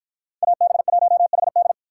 Morse code, heard as a keyed tone, about 53 wpm, 700 Hz. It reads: AB1HD